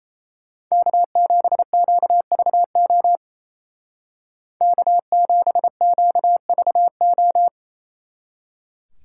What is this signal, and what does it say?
Morse code, heard as a keyed tone, about 33 wpm, 700 Hz. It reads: K7QVO K7QVO
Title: Speed Racing - USA 1x3 Call Signs 28wpm